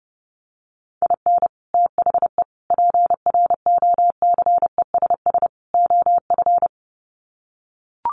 Morse code, heard as a keyed tone, about 30 wpm, 700 Hz. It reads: IN THE PROCESS OF
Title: Sets of 4 Words 30wpm